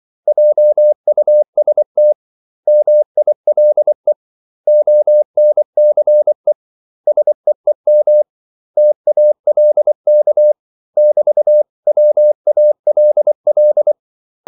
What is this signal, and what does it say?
Morse code, heard as a keyed tone, about 24 wpm, 600 Hz. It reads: JUST MILE ONCE SEEM TALK <BT>WALL
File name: CW / Practice / 500MostCommonEnglishWords24